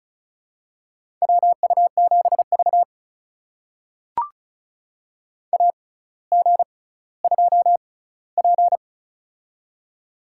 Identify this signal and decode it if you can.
Morse code, heard as a keyed tone, about 36 wpm, spaced as for 18 wpm, 700 Hz. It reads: WU7V AG2P
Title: USA Call Signs - 4 Characters - 6x Character Spacing - 35wpm